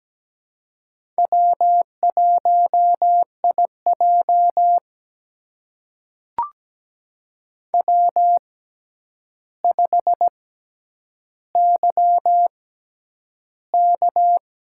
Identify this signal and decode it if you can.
Morse code, heard as a keyed tone, about 17 wpm, spaced as for 9 wpm, 700 Hz. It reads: W1IJ W5YK
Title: USA Call Signs - 6x Character Spacing - 17wpm